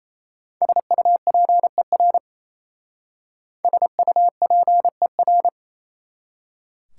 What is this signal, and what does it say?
Morse code, heard as a keyed tone, about 35 wpm, 700 Hz. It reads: SUPER SUPER
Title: Speed-Racing Prefixes and Suffixes 28wpm